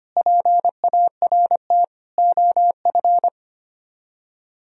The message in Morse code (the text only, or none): PART OF